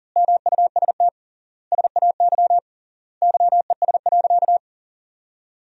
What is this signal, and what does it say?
Morse code, heard as a keyed tone, about 40 wpm, 700 Hz. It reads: MUST SAY YES.
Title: 3x Word-Spacing - Sentences from Top 500 Words - 40wpm